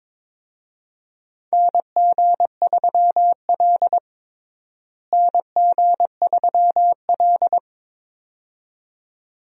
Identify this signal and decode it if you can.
Morse code, heard as a keyed tone, about 22 wpm, 700 Hz. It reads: NG3L NG3L
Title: USA Call Signs - 4 Characters - Repeated 2x - 22wpm